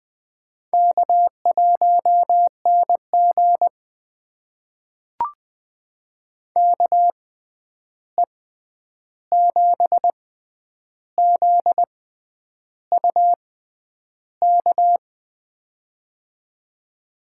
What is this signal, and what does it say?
Morse code, heard as a keyed tone, about 20 wpm, spaced as for 9 wpm, 700 Hz. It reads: K1NG KE7ZUK
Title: USA Call Signs - 6x Character Spacing - 20wpm